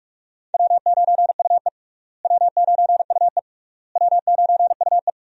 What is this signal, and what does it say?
Morse code, heard as a keyed tone, about 41 wpm, 700 Hz. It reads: W9UE W9UE W9UE
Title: USA Call Signs - 4 Characters - Repeated 3x - 45wpm